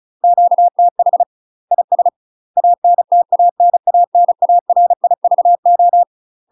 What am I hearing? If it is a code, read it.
Morse code, heard as a keyed tone, about 34 wpm, 700 Hz. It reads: QTH IS ANTANANARIVO
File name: UK MASTER QSOs TRAINING / 35WPM / UK MASTER QSOs 0037